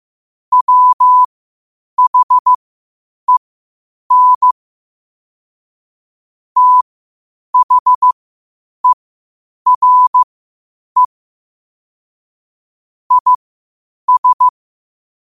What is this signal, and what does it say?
Morse code, heard as a keyed tone, about 15 wpm, spaced as for 8 wpm, 1000 Hz. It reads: WHEN THERE IS
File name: qotdfiles / 8wpm